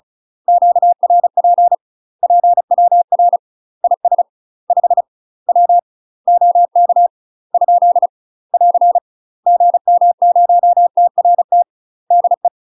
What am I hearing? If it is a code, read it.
Morse code, heard as a keyed tone, about 35 wpm, 700 Hz. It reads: QRP PWR IS 5 W OK ? <AR> GM0TRT DE
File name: UK MASTER QSOs TRAINING / 35WPM / UK MASTER QSOs 0007